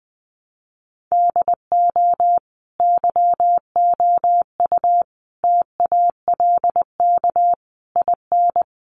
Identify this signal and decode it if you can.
Morse code, heard as a keyed tone, about 20 wpm, 700 Hz. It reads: DO YOU TALK IN
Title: Sentences from Top 500 Words 20wpm